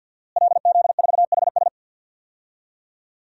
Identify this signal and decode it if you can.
Morse code, heard as a keyed tone, about 50 wpm, 700 Hz. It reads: LZ4HS
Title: DXCC-100 Call Signs - 5 Characters - Repeated 3x - 50wpm